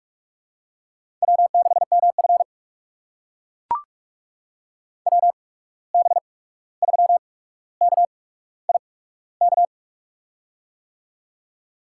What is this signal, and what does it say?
Morse code, heard as a keyed tone, about 44 wpm, spaced as for 17 wpm, 700 Hz. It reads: W6MF WB3KIK
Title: USA Call Signs - 8x Character Spacing - 45wpm